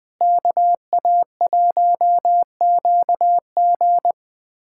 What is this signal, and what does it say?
Morse code, heard as a keyed tone, about 20 wpm, 700 Hz. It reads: KA1QG
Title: USA Callsign DE Callsign - 5 Characters - 20wpm